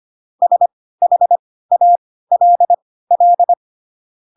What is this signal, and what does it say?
Morse code, heard as a keyed tone, about 25 wpm, spaced as for 16 wpm, 700 Hz. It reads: SHALL
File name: Pinocchio / Pinocchio ch5 10 25 ik1hge